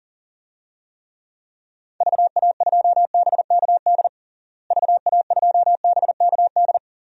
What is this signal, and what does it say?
Morse code, heard as a keyed tone, about 40 wpm, 700 Hz. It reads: VA2BKD VA2BKD
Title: DXCC-100 Call Signs - 6 Characters - Repeated 2x - 40wpm